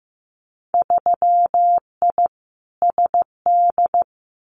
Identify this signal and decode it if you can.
Morse code, heard as a keyed tone, about 15 wpm, 700 Hz. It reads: 3I SD
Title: Winter Field Day Exchanges Encore 15wpm